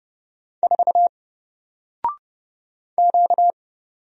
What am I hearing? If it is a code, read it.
Morse code, heard as a keyed tone, about 30 wpm, 700 Hz. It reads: 4 Q
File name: Single-Letter-Number-30wpm